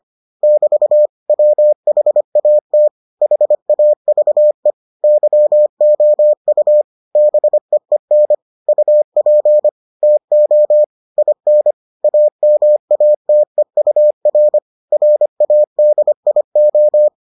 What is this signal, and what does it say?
Morse code, heard as a keyed tone, about 25 wpm, 600 Hz. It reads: <BT> WHAT HAVE YOU BEEN UP TO IN AMATEUR RADIO